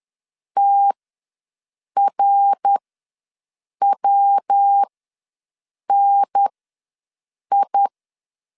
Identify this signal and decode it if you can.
Morse code, heard as a keyed tone, about 10 wpm, spaced as for 6 wpm, 800 Hz. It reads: TRWNI
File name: LEZIONI / 6 PKX0 / 6 PKX ws 10wpm 3706